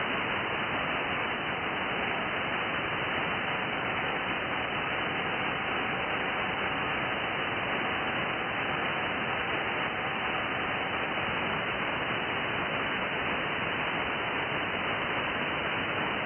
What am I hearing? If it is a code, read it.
FT8.
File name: IC-7100